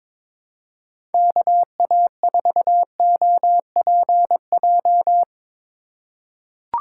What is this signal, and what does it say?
Morse code, heard as a keyed tone, about 22 wpm, 700 Hz. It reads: KA4OPJ
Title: USA Call Signs - 8x Character Spacing - 22wpm